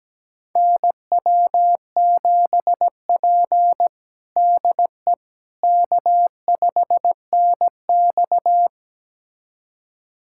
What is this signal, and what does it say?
Morse code, heard as a keyed tone, about 17 wpm, 700 Hz. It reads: NW7P DE K5NX
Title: USA Callsign DE Callsign - 4 Characters - 17wpm